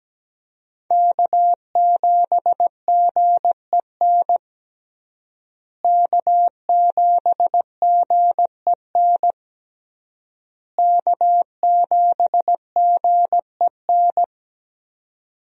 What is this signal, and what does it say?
Morse code, heard as a keyed tone, about 17 wpm, 700 Hz. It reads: K7GEN K7GEN K7GEN
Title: USA Call Signs - 5 Characters - Repeated 3x - 17wpm